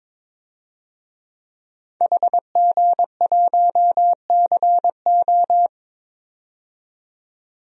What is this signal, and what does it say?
Morse code, heard as a keyed tone, about 22 wpm, 700 Hz. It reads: HG1CO
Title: DXCC-100 Call Signs - 5 Characters - 22wpm